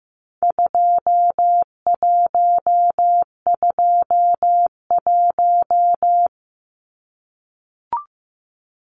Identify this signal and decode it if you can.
Morse code, heard as a keyed tone, about 15 wpm, 700 Hz. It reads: 2121
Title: Numbers - 4-Digit 15wpm